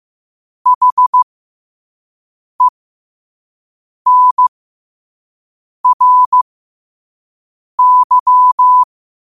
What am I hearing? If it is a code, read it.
Morse code, heard as a keyed tone, about 15 wpm, spaced as for 5 wpm, 1000 Hz. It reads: HENRY
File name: qotdfiles / 5wpm